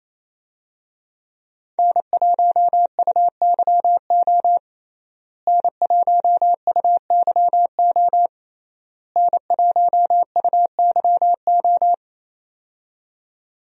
Morse code, heard as a keyed tone, about 28 wpm, 700 Hz. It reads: N1UYO N1UYO N1UYO
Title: USA Call Signs - 5 Characters - Repeated 3x - 28wpm